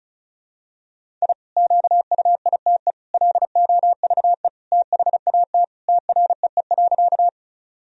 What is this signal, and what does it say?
Morse code, heard as a keyed tone, about 36 wpm, 700 Hz. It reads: I QUITE LOVE THAT TREE.
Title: Sentences from Top 1000 Words - Review Part 1 - 35wpm